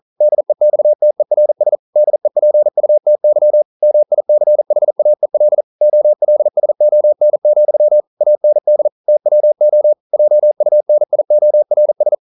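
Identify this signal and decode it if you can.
Morse code, heard as a keyed tone, about 42 wpm, 600 Hz. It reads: DEXTERS DEPUTY MICHAEL OLSON, AND TWO JUNIORS
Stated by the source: Morse code practice at 41 wpm, from CW med Gnister.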